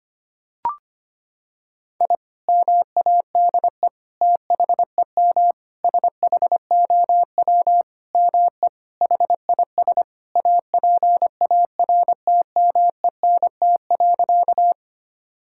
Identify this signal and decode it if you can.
Morse code, heard as a keyed tone, about 25 wpm, 700 Hz. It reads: I MADE THEM SHOW ME HIS APARTMENT.
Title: Sentences from Top 1000 Words - Review Part 6 - 25wpm